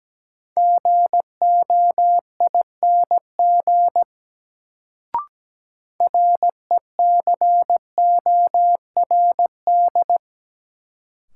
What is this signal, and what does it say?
Morse code, heard as a keyed tone, about 17 wpm, 700 Hz. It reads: GOING RECORD
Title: Sentences from Top 500 Words - New Word Drill - 17wpm